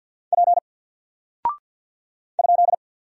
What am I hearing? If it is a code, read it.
Morse code, heard as a keyed tone, about 48 wpm, 700 Hz. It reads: P ?
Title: Single Character (Letter:Number:Punctuation:Prosign) Alt 50wpm